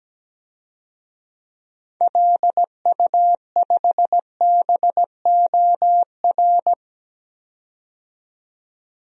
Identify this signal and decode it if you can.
Morse code, heard as a keyed tone, about 17 wpm, 700 Hz. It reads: LU5BOR
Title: DXCC-100 Call Signs - 6 Characters - 17wpm